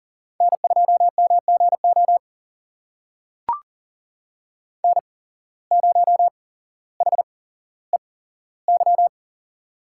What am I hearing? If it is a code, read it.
Morse code, heard as a keyed tone, about 38 wpm, spaced as for 17 wpm, 700 Hz. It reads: N2MGO N0HEY
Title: USA Call Signs - 5 Characters - 8x Character Spacing - 40wpm